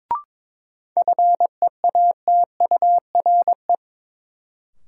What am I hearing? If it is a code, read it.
Morse code, heard as a keyed tone, about 22 wpm, 700 Hz. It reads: FEATURE
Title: Sentences from Top 800 Words - New Word Drill - 22wpm